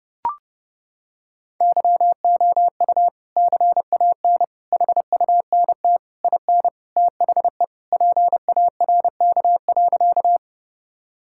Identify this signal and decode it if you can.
Morse code, heard as a keyed tone, about 30 wpm, 700 Hz. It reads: YOU CAN HUNT IN THE PARK.
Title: Sentences from Top 1600 Words 30wpm